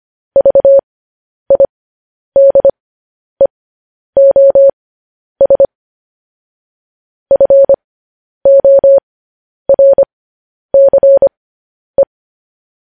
Morse code, heard as a keyed tone, about 25 wpm, spaced as for 10 wpm, 550 Hz. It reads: VIDEOS FORCE
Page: News Headlines in Morse Code at 10 WPM